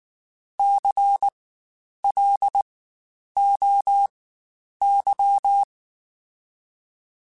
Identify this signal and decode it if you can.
Morse code, heard as a keyed tone, about 19 wpm, spaced as for 9 wpm, 800 Hz. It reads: CLOY
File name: Spaced Alphas / Spaced Alphas mp3s / Cs sp @18